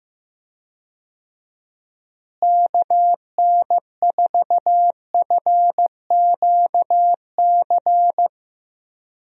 Morse code, heard as a keyed tone, about 15 wpm, 700 Hz. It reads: KN4FQC
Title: USA Call Signs - 6 Characters - Repeated 2x - 15wpm